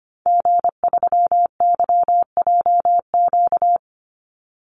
Morse code, heard as a keyed tone, about 25 wpm, 700 Hz. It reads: G3YJQ
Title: Call Signs - 25wpm